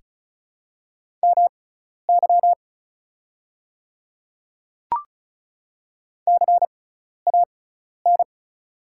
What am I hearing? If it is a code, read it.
Morse code, heard as a keyed tone, about 36 wpm, spaced as for 12 wpm, 700 Hz. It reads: MY CAN
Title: 6x Character Spacing - Top 100 Words - 35wpm